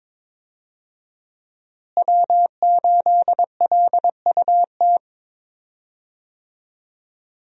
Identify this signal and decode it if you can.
Morse code, heard as a keyed tone, about 22 wpm, 700 Hz. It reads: W8LUT
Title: USA Call Signs - 5 Characters - Repeated 3x - 22wpm